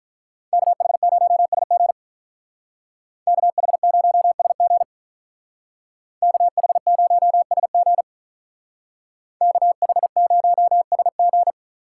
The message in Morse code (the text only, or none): KH0SG KH0SG KH0SG KH0SG